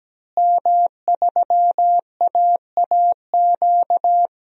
Morse code, heard as a keyed tone, about 17 wpm, 700 Hz. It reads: M3AAQ
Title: DXCC-100 Call Signs - 5 Characters - Repeated 3x - 17wpm